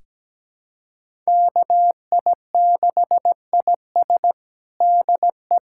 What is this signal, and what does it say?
Morse code, heard as a keyed tone, about 17 wpm, 700 Hz. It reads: KI6IS DE
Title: USA Callsign DE Callsign - 5 Characters - 17wpm